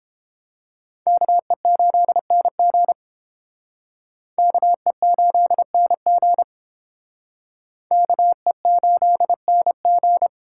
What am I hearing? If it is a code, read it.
Morse code, heard as a keyed tone, about 31 wpm, 700 Hz. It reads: KE8NG KE8NG KE8NG